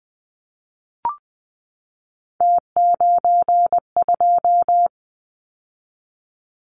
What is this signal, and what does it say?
Morse code, heard as a keyed tone, about 20 wpm, 700 Hz. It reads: T92
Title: Callsign Prefixes for Top 50 DXCC - 20wpm